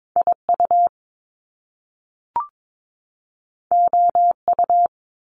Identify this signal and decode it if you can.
Morse code, heard as a keyed tone, about 22 wpm, 700 Hz. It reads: IU OU
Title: Top Two Letter Combinations in Words 22wpm